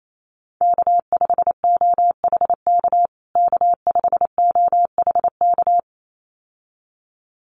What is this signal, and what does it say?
Morse code, heard as a keyed tone, about 28 wpm, 700 Hz. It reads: K5OHK K5OHK
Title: USA Call Signs Repeated 2x 28wpm